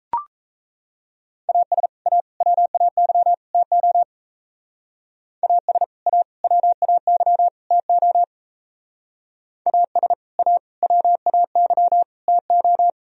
Morse code, heard as a keyed tone, about 37 wpm, 700 Hz. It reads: AS A WAY TO AS A WAY TO AS A WAY TO